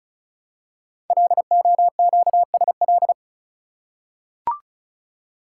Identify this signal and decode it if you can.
Morse code, heard as a keyed tone, about 36 wpm, 700 Hz. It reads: LOQSL
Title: Code Groups - 5 Letters - 35wpm